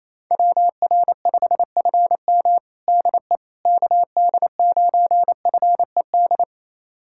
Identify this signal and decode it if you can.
Morse code, heard as a keyed tone, about 28 wpm, 700 Hz. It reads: WR5FM DE KD9FED
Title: USA Callsign DE Callsign 28wpm